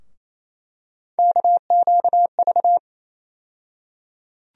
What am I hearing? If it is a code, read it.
Morse code, heard as a keyed tone, about 28 wpm, 700 Hz. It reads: KQV
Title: USA Call Sign Postfixes - 3 Letters 28wpm